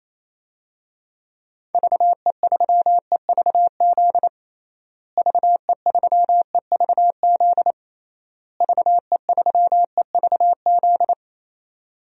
Morse code, heard as a keyed tone, about 28 wpm, 700 Hz. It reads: VE3EVZ VE3EVZ VE3EVZ
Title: DXCC-100 Call Signs - 6 Characters - Repeated 3x - 28wpm